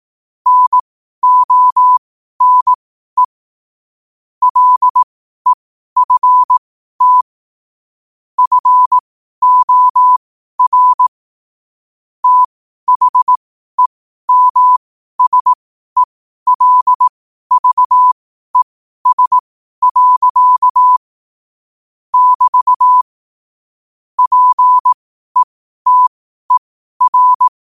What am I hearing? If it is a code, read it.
Morse code, heard as a keyed tone, about 18 wpm, spaced as for 12 wpm, 1000 Hz. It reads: NONE LEFT FOR THEMSELVES. <BT> PETER
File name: qotdfiles / 12wpm